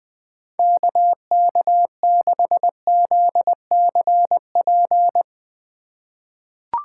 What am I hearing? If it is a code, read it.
Morse code, heard as a keyed tone, about 20 wpm, 700 Hz. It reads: KK6ZCP
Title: USA Call Signs - 6 Characters - 4x Character Spacing - 20wpm